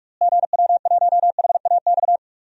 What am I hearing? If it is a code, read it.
Morse code, heard as a keyed tone, about 46 wpm, 700 Hz. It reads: GW1HAX
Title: DXCC-100 Call Signs - 6 Characters - Repeated 3x - 45wpm